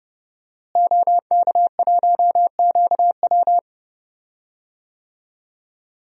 Morse code, heard as a keyed tone, about 31 wpm, 700 Hz. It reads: OK1QW
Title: DXCC-100 Call Signs - 5 Characters - Repeated 2x - 30wpm